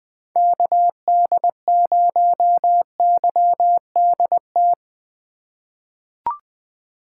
Morse code, heard as a keyed tone, about 20 wpm, 700 Hz. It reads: KD0YDT
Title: USA Call Signs - 6 Characters - 2x Character Spacing - 20wpm